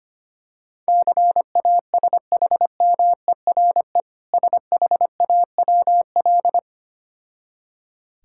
Morse code, heard as a keyed tone, about 25 wpm, 690 Hz. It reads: CASHMERE SHAWL
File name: practice / 25wpm / Difficult Words with Hints - 25wpm